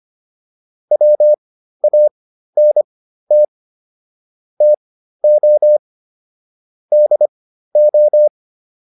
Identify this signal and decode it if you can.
Morse code, heard as a keyed tone, about 25 wpm, spaced as for 13 wpm, 600 Hz. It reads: WANT TO DO